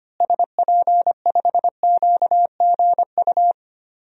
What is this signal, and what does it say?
Morse code, heard as a keyed tone, about 25 wpm, 700 Hz. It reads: SP5QGU
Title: DXCC-100 Call Signs - 6 Characters - Repeated 3x - 25wpm